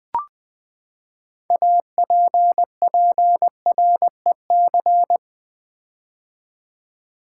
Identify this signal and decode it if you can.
Morse code, heard as a keyed tone, about 20 wpm, 700 Hz. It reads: APPREC
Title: Build A Word - 6 Letters At A Time - 12-Letter Words - 20wpm